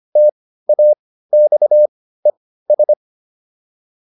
Morse code, heard as a keyed tone, about 25 wpm, spaced as for 15 wpm, 600 Hz. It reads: TAXES